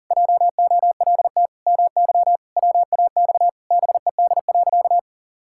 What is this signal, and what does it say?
Morse code, heard as a keyed tone, about 40 wpm, 700 Hz. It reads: JOLT MY WAX BED.
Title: Pangram Challenge v2 - 40wpm